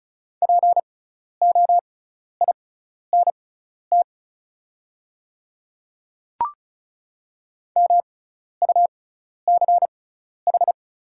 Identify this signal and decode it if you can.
Morse code, heard as a keyed tone, about 36 wpm, spaced as for 12 wpm, 690 Hz. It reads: POINT MUCH
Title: 6x Character Spacing - Top 200 Words - 35wpm